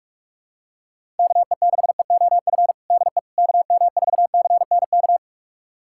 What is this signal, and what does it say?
Morse code, heard as a keyed tone, about 44 wpm, 700 Hz. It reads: KE6EOF DE KM4CNK
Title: USA Callsign DE Callsign - 6 Characters - 45wpm